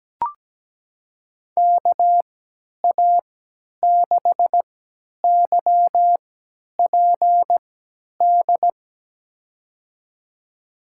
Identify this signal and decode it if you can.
Morse code, heard as a keyed tone, about 17 wpm, spaced as for 10 wpm, 700 Hz. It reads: KA6YPD
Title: USA Call Signs - 6 Characters - 3x Character Spacing - 17wpm